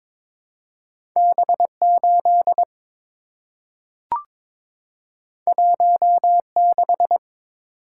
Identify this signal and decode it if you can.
Morse code, heard as a keyed tone, about 22 wpm, 700 Hz. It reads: B8 16
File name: practice / 22wpm / Two Letter-Number Combinations - 22wpm